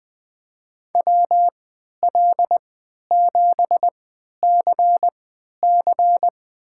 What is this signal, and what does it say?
Morse code, heard as a keyed tone, about 20 wpm, spaced as for 11 wpm, 700 Hz. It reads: WL7CC
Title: USA Call Signs - 5 Characters - 3x Character Spacing - 20wpm